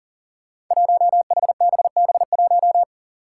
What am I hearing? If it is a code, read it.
Morse code, heard as a keyed tone, about 40 wpm, 700 Hz. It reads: 1HBB1
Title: Five Challenging Letter-Number Combinations 40wpm